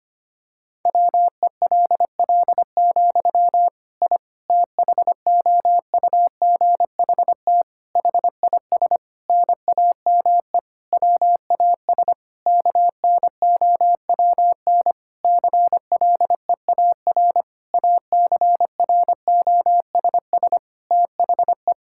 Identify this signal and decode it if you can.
Morse code, heard as a keyed tone, about 25 wpm, 700 Hz. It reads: WELL, I THOUGHT HIS NAME WAS KNOWN CLEAR ACROSS THE
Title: Sentences from Top 1000 Words - Review Part 5 - 25wpm